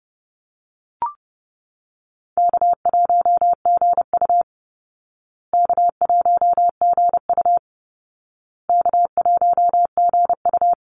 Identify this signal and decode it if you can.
Morse code, heard as a keyed tone, about 30 wpm, 700 Hz. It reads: K1GU K1GU K1GU
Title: Easy Call Signs Repeated 3x - 30wpm